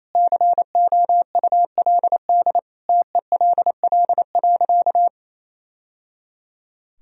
Morse code, heard as a keyed tone, about 28 wpm, 700 Hz. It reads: COULD TELL.
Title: Sentences from Top 1000 Words - Review Part 4 - 28wpm